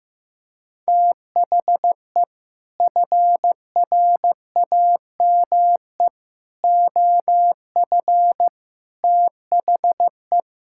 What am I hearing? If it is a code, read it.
Morse code, heard as a keyed tone, about 14 wpm, 700 Hz. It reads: THE FRAME OF THE
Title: Sentences from Top 1000 Words - Review Part 6 - 15wpm